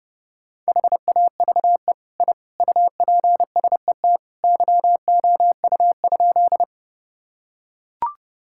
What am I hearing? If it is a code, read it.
Morse code, heard as a keyed tone, about 30 wpm, 700 Hz. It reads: HAVE I UPSET YOU?
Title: Sentences from Top 1000 Words - Review Part 5 - 30wpm